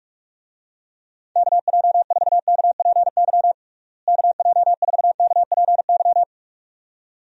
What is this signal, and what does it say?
Morse code, heard as a keyed tone, about 44 wpm, 700 Hz. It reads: KJ4KPY KJ4KPY
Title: USA Call Signs - 6 Characters - Repeated 2x - 45wpm